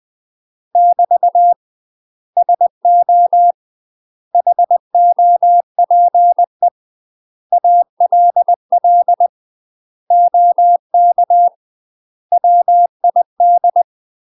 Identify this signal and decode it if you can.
Morse code, heard as a keyed tone, about 20 wpm, 700 Hz. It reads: <BT> SO HOPE ALL OK WID
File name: UK MASTER QSOs TRAINING/20WPM